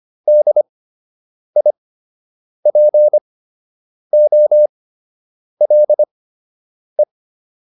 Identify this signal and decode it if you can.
Morse code, heard as a keyed tone, about 25 wpm, spaced as for 8 wpm, 600 Hz. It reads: DIPOLE